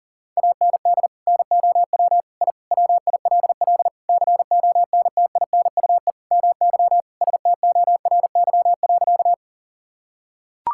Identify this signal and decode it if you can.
Morse code, heard as a keyed tone, about 40 wpm, 700 Hz. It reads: AND NOW I WILL CONTINUE MY STORY.
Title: Sentences from Top 1000 Words - Review Part 6 - 40wpm